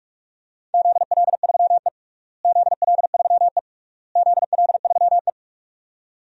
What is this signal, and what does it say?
Morse code, heard as a keyed tone, about 47 wpm, 700 Hz. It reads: ZL3E ZL3E ZL3E
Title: DXCC-100 Call Signs - 4 Characters - Repeated 3x - 45wpm